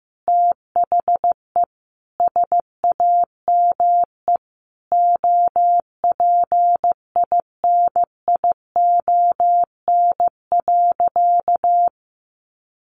Morse code, heard as a keyed tone, about 15 wpm, 700 Hz. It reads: THE SAME OPINION.
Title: Sentences from Top 1200 Words 15wpm